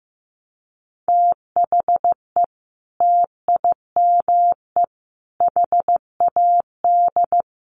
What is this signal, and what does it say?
Morse code, heard as a keyed tone, about 15 wpm, 700 Hz. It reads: THE TIME HAD
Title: Sentences from Top 100 Words 15wpm